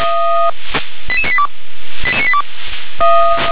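Five-tone selective call (ZVEI), heard as quick runs of five tones, tones of about 70 ms. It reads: 88982
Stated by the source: ZVEI five-tone selective call received on an SDR.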